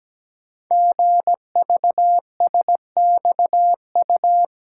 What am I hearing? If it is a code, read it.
Morse code, heard as a keyed tone, about 17 wpm, 700 Hz. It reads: GVSXU
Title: Code Groups - 5 Letters - 17wpm